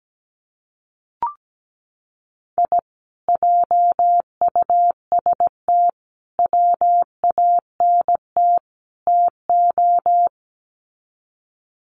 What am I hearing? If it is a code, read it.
Morse code, heard as a keyed tone, about 17 wpm, 700 Hz. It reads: I JUST WANT TO
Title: Sets of 4 Words 17wpm